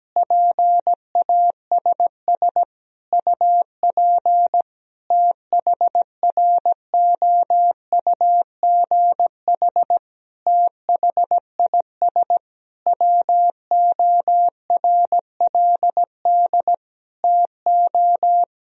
Morse code, heard as a keyed tone, about 17 wpm, 700 Hz. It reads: PASS UP THROUGH THIS WORLD TO
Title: Sentences from Top 1000 Words - Review Part 2 - 17wpm